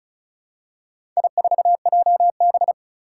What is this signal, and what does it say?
Morse code, heard as a keyed tone, about 36 wpm, 700 Hz. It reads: I4JB
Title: Four Challenging Letter-Number Combinations 35wpm